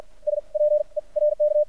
Morse code, heard as a keyed tone, about 53 wpm, spaced as for 40 wpm, 600 Hz. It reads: SPE RK
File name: archived / CRSnet